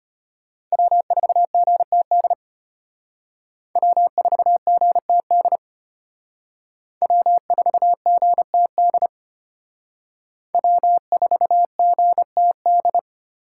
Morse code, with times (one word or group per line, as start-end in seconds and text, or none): 0.7-2.3 W4GTD
3.8-5.6 W4GTD
7.0-9.1 W4GTD
10.5-13.0 W4GTD